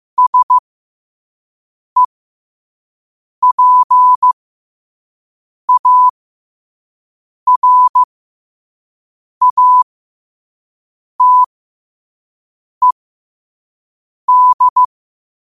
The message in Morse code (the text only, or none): SEPARATED